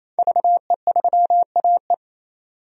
Morse code, heard as a keyed tone, about 28 wpm, 700 Hz. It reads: VE3AE
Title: DXCC-100 Call Signs - 5 Characters - 28wpm